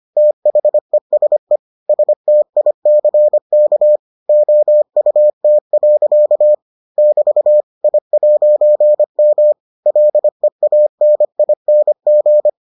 Morse code, heard as a keyed tone, about 25 wpm, 600 Hz. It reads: THESE STICK OUT. <BT> I'M LEANING